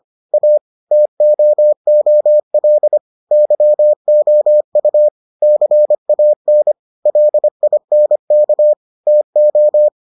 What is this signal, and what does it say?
Morse code, heard as a keyed tone, about 25 wpm, 600 Hz. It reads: A TOOL YOU CAN LINK TO